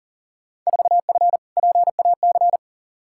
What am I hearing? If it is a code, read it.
Morse code, heard as a keyed tone, about 40 wpm, 700 Hz. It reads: 4F PAC
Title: Field Day Exchanges Encore - 40wpm